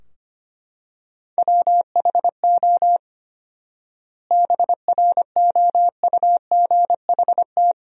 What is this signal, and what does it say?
Morse code, heard as a keyed tone, about 25 wpm, 700 Hz. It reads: WHO BROUGHT
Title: 4x Word-Spacing - Sentences from Top 500 Words - 25wpm